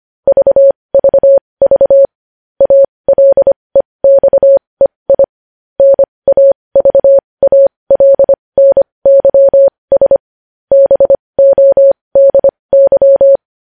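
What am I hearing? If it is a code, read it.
Morse code, heard as a keyed tone, about 25 wpm, spaced as for 20 wpm, 550 Hz. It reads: VVV ALEXEI NAVALNYS BODY
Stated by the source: Morse news headlines sent at 20.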